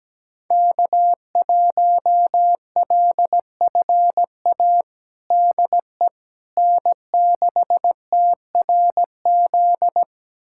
Morse code, heard as a keyed tone, about 17 wpm, 700 Hz. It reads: K1LFA DE N6TRZ
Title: USA Callsign DE Callsign - 5 Characters - 17wpm